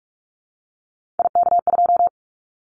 Morse code, heard as a keyed tone, about 44 wpm, 700 Hz. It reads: IK2
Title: Callsign Prefixes for Top 50 DXCC - 45wpm